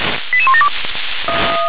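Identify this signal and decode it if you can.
Five-tone selective call (ZVEI), heard as quick runs of five tones, tones of about 70 ms. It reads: 88172
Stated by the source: ZVEI five-tone selective call received by SDR.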